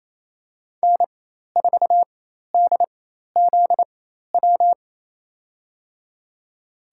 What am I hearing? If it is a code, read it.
Morse code, heard as a keyed tone, about 28 wpm, spaced as for 13 wpm, 700 Hz. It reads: N4DZW
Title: USA Call Signs - 5 Characters - 4x Character Spacing - 28wpm